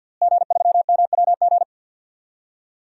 Morse code, heard as a keyed tone, about 51 wpm, 700 Hz. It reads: G3MWG
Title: DXCC-100 Call Signs - 5 Characters - Repeated 3x - 50wpm